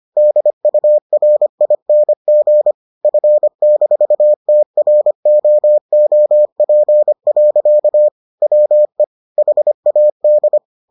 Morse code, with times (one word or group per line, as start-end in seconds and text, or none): 0.2-2.7 DURING
3.0-8.1 F-TROOP.
8.4-9.0 WE
9.4-10.6 HAD